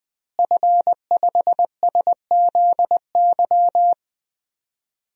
Morse code, heard as a keyed tone, about 21 wpm, 700 Hz. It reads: F5SZY
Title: DXCC-100 Call Signs - 5 Characters - Repeated 3x - 20wpm